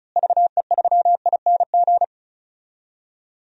Morse code, heard as a keyed tone, about 36 wpm, 700 Hz. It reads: VE3ING
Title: DXCC-100 Call Signs - 6 Characters - Repeated 3x - 35wpm